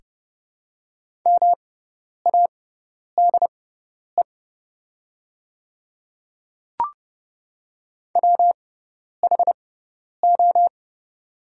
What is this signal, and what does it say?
Morse code, heard as a keyed tone, about 30 wpm, spaced as for 10 wpm, 700 Hz. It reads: MADE WHO
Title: 6x Character Spacing - Top 200 Words - 30wpm